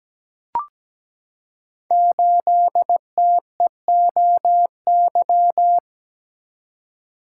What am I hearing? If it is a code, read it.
Morse code, heard as a keyed tone, about 17 wpm, 700 Hz. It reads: 8TEOY
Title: Code Groups - 5 Letter:Numbers - 17wpm